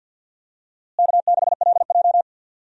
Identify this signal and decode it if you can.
Morse code, heard as a keyed tone, about 47 wpm, 700 Hz. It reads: K6LJ
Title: USA Call Signs - 4 Characters - Repeated 3x - 50wpm